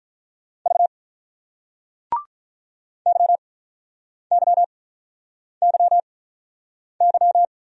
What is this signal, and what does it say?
Morse code, heard as a keyed tone, about 44 wpm, spaced as for 8 wpm, 700 Hz. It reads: V YYYY
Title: Speed-Racing Single Letters 35wpm